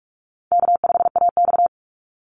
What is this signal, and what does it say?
Morse code, heard as a keyed tone, about 44 wpm, 700 Hz. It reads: K5AX
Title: Call Signs - 45wpm